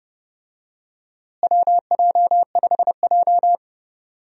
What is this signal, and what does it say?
Morse code, heard as a keyed tone, about 30 wpm, 700 Hz. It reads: WJ5J
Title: USA Call Signs - 4 Characters - Repeated 2x - 30wpm